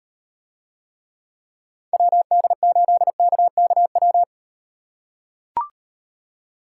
Morse code, heard as a keyed tone, about 38 wpm, 700 Hz. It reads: WD8KKW